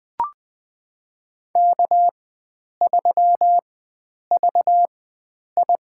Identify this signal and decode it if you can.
Morse code, heard as a keyed tone, about 20 wpm, spaced as for 9 wpm, 700 Hz. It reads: K3VI
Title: USA Call Signs - 4x Character Spacing - 20wpm